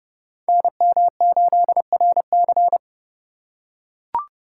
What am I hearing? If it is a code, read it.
Morse code, heard as a keyed tone, about 28 wpm, 700 Hz. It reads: NM8RC